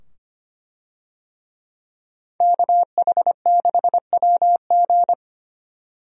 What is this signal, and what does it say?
Morse code, heard as a keyed tone, about 25 wpm, 700 Hz. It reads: KH6WG
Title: DXCC-100 Call Signs - 5 Characters - 25wpm